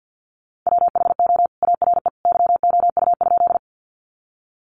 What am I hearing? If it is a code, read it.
Morse code, heard as a keyed tone, about 50 wpm, 700 Hz. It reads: WHO ARE YOU?
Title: Sentences from Top 100 Words 50wpm